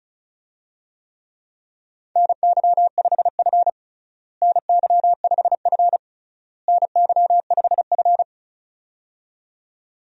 Morse code, heard as a keyed tone, about 36 wpm, 700 Hz. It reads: NY5F NY5F NY5F
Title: USA Call Signs - 4 Characters - Repeated 3x - 35wpm